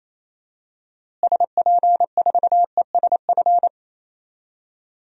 Morse code, heard as a keyed tone, about 28 wpm, 700 Hz. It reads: SP4ESF
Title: DXCC-100 Call Signs - 6 Characters - Repeated 3x - 28wpm